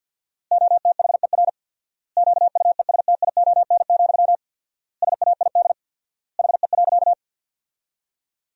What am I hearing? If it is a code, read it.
Morse code, heard as a keyed tone, about 50 wpm, 700 Hz. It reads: OTHER QUESTION, SAID HE.
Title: 4x Word-Spacing - Sentences from Top 400 Words - 50wpm